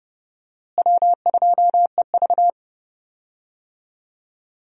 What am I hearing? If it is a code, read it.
Morse code, heard as a keyed tone, about 30 wpm, 700 Hz. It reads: W2EV
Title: USA Call Signs - 4 Characters - Repeated 2x - 30wpm